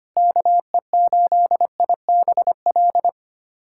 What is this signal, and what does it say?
Morse code, heard as a keyed tone, about 25 wpm, 700 Hz. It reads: KE8IBL
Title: USA Callsign DE Callsign - 6 Characters - 25wpm